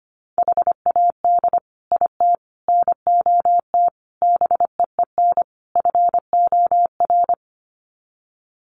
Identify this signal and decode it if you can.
Morse code, heard as a keyed tone, about 25 wpm, 700 Hz. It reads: HAD IT NOT BEEN FOR
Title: Sets of 5 Words Encore2 25wpm